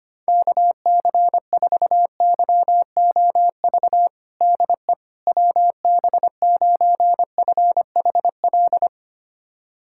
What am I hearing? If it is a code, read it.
Morse code, heard as a keyed tone, about 25 wpm, 700 Hz. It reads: KC4YOV DE WB9FHL
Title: USA Callsign DE Callsign - 6 Characters - 25wpm